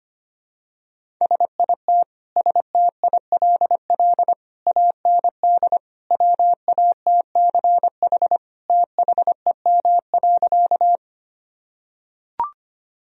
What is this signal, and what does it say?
Morse code, heard as a keyed tone, about 25 wpm, 700 Hz. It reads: SIT STILL AND WATCH THEM.
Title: Sentences from Top 1000 Words - Review Part 1 - 25wpm